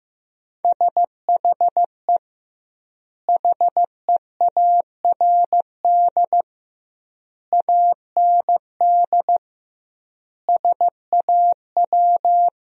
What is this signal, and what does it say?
Morse code, heard as a keyed tone, about 15 wpm, 700 Hz. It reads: SHE HEARD AND SAW
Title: 2x Word-Spacing - Sentences from Top 500 Words - 15wpm